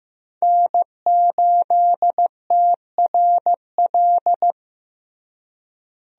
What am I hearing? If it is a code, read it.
Morse code, heard as a keyed tone, about 15 wpm, 700 Hz. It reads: N8TRL
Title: USA Call Signs - 5 Characters - Repeated 2x - 15wpm